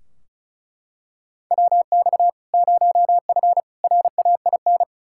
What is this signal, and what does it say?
Morse code, heard as a keyed tone, about 36 wpm, 700 Hz. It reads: WX 0F RAIN
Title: Practice QSO - Exchange 2 - Info - 35wpm